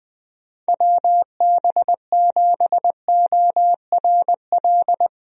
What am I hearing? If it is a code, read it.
Morse code, heard as a keyed tone, about 20 wpm, 700 Hz. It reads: WB7ORL
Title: USA Call Signs - 6 Characters - Repeated 3x - 20wpm